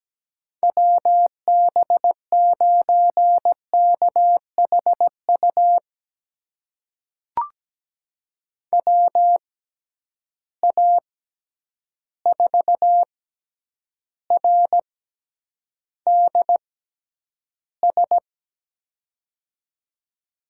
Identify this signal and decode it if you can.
Morse code, heard as a keyed tone, about 17 wpm, spaced as for 9 wpm, 700 Hz. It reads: WB9KHU WA4RDS
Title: USA Call Signs - 6 Characters - 6x Character Spacing - 17wpm